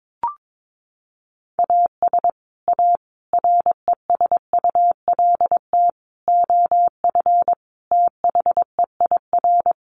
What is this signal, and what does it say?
Morse code, heard as a keyed tone, about 22 wpm, 700 Hz. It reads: AS A RESULT OF THEIR
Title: Sets of 5 Words Encore2 22wpm